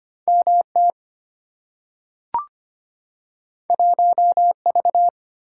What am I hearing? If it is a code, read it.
Morse code, heard as a keyed tone, about 25 wpm, 700 Hz. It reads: MT 1V
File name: Two Letter-Number Combinations - 25wpm